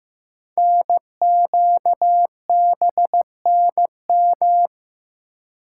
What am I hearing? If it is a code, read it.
Morse code, heard as a keyed tone, about 15 wpm, 700 Hz. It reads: NQBNM
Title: Code Groups - 5 Letters - 15wpm